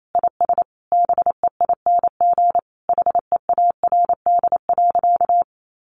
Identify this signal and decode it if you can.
Morse code, heard as a keyed tone, about 28 wpm, 700 Hz. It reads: IS BEING HEARD.
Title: Sentences from Top 1500 Words 28wpm